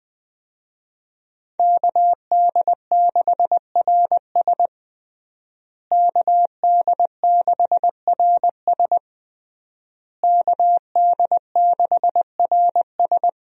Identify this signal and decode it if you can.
Morse code, heard as a keyed tone, about 20 wpm, 700 Hz. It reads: KD6RS KD6RS KD6RS
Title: USA Call Signs - 5 Characters - Repeated 3x - 20wpm